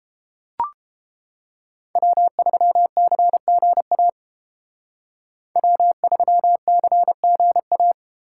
Morse code, heard as a keyed tone, about 30 wpm, 700 Hz. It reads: W3CGA W3CGA